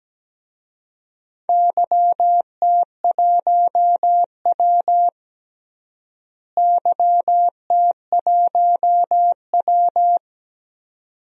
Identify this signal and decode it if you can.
Morse code, heard as a keyed tone, about 17 wpm, 700 Hz. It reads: YT1W YT1W
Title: DXCC-100 Call Signs - 4 Characters - Repeated 2x - 17wpm